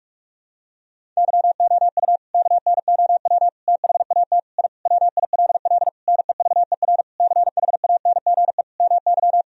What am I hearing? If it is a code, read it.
Morse code, heard as a keyed tone, about 41 wpm, 700 Hz. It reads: YOU KNOW THAT I WILL NEVER CHANGE MY
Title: Sentences from Top 1000 Words - Review Part 1 - 45wpm